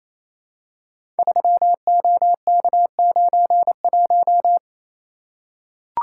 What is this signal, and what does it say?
Morse code, heard as a keyed tone, about 29 wpm, 700 Hz. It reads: 3OK91
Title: Code Groups - 5 Letter:Numbers - 28wpm